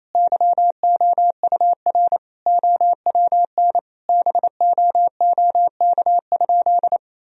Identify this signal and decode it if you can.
Morse code, heard as a keyed tone, about 27 wpm, 700 Hz. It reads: YOUR OWN BOOK?
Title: Sentences from Top 1000 Words - Review Part 3 - 28wpm